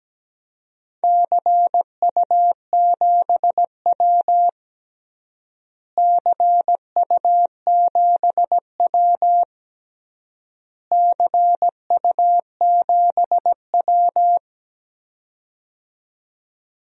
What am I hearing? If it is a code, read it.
Morse code, heard as a keyed tone, about 17 wpm, 700 Hz. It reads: CU7W CU7W CU7W
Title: DXCC-100 Call Signs - 4 Characters - Repeated 3x - 17wpm